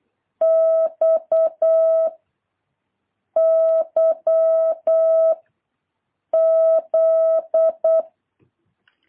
Morse code, heard as a keyed tone, about 8 wpm, 650 Hz. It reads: X Y Z
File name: jh8cbh / 2023cw